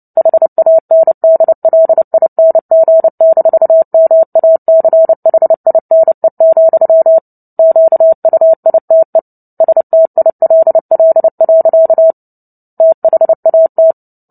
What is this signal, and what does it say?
Morse code, heard as a keyed tone, about 30 wpm, 650 Hz. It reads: HANDLING-MACHINE, QUITE STILL. THAT
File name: Book Two / 03-Chapter 3 - 29 WPM